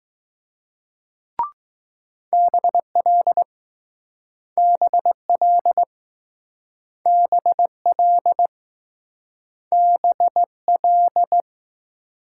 Morse code, heard as a keyed tone, about 20 wpm, 700 Hz. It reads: BL BL BL BL